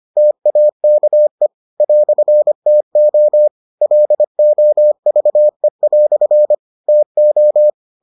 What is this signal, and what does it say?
Morse code, heard as a keyed tone, about 25 wpm, 600 Hz. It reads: TAKE "TO LOVE" TO